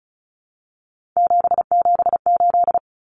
Morse code, heard as a keyed tone, about 34 wpm, 700 Hz. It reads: 778
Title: Numbers - 3-Digit 35wpm